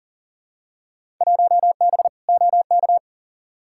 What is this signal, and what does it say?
Morse code, heard as a keyed tone, about 39 wpm, 700 Hz. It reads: 1B OK
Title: Field Day Exchanges Encore - 40wpm